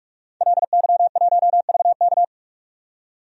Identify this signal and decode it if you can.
Morse code, heard as a keyed tone, about 44 wpm, 690 Hz. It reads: LY1VK
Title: DXCC-100 Call Signs - 5 Characters - 45wpm